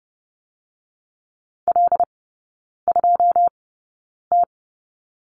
Morse code, heard as a keyed tone, about 30 wpm, spaced as for 9 wpm, 700 Hz. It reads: L2T
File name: Lesson 16e - Letter c - Three Characters 30wpm